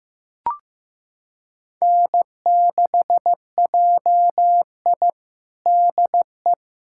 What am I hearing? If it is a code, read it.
Morse code, heard as a keyed tone, about 15 wpm, 700 Hz. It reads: N6JI DE
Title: USA Callsign DE Callsign - 4 Characters - 15wpm